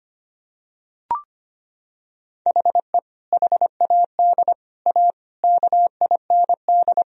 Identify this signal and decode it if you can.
Morse code, heard as a keyed tone, about 25 wpm, 700 Hz. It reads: HE HAD A KIND
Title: Sentences from Top 1000 Words - Review Part 2 - 25wpm